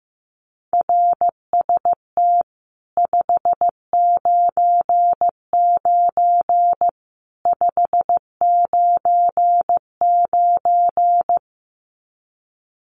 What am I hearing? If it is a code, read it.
Morse code, heard as a keyed tone, about 15 wpm, 700 Hz. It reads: RST 599 599
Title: Common QSO Phrases 15wpm